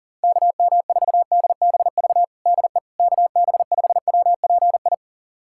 Morse code, heard as a keyed tone, about 40 wpm, 700 Hz. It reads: KM4DBV DE KB5WPI
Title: USA Callsign DE Callsign - 6 Characters - 40wpm